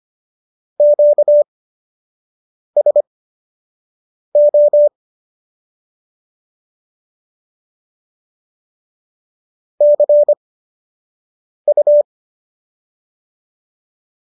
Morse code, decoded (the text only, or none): QSO CU